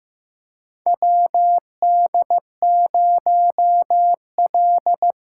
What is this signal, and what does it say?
Morse code, heard as a keyed tone, about 15 wpm, 700 Hz. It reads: WD0L
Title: USA Callsign DE Callsign - 4 Characters - 15wpm